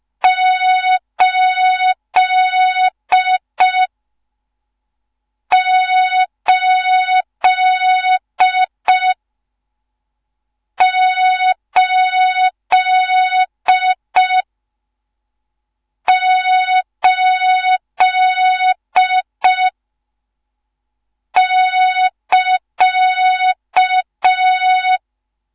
Morse code, heard as a keyed tone, about 5 wpm, 3700 Hz. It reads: 8 8 8 8 <KA>
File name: Morze-HA2MN / hang / 14 280 HA2MN